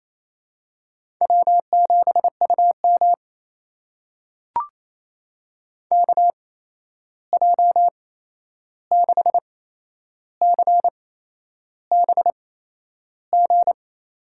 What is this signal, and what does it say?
Morse code, heard as a keyed tone, about 28 wpm, spaced as for 11 wpm, 700 Hz. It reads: W7UM KJ6CBG
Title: USA Call Signs - 8x Character Spacing - 28wpm